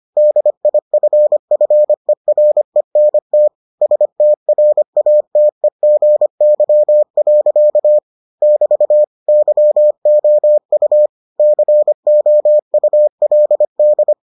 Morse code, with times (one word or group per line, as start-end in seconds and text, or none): 0.2-3.5 DIFFERENT
3.8-8.0 STRATEGY.
8.4-9.0 <BT>
9.3-11.1 YOU
11.4-14.1 COULD